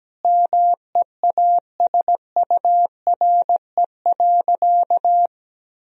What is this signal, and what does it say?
Morse code, heard as a keyed tone, about 17 wpm, 700 Hz. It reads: MEASURE.
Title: Sentences from Top 1000 Words - Review Part 1 - 17wpm